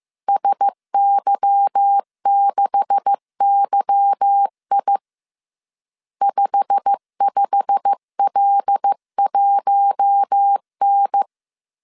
Morse code, heard as a keyed tone, about 15 wpm, 800 Hz. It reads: SY6YI 55L1N
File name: CW / DET / 1 9 1500C S A 14wpm 3222